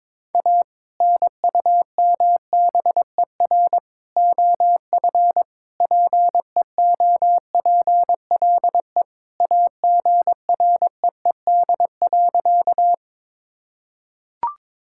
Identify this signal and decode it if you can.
Morse code, heard as a keyed tone, about 22 wpm, 700 Hz. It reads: A NUMBER OF PEOPLE AGREED.
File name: practice / 22wpm / Sentences with 3-Word Phrases - 22wpm